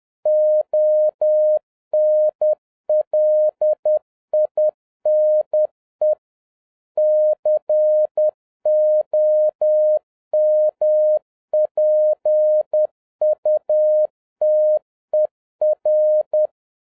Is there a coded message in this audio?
Yes: Morse code from 0.3 to 16.5 s, heard as a keyed tone, about 10 wpm, 610 Hz. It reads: ONLINE COMPUTER